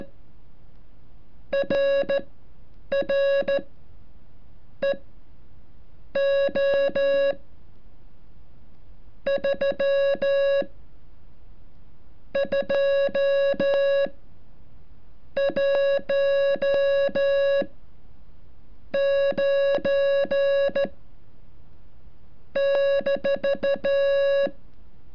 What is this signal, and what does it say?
Morse code, heard as a keyed tone, about 14 wpm, spaced as for 7 wpm, 560 Hz. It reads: RR EO 3 2 1 9 -
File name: curso / ejercicio 33